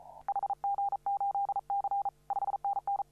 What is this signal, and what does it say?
Morse code, heard as a keyed tone, about 37 wpm, 800 Hz. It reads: HG8C 5NN
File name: CW / RHOB CW 2017 HA6P 0001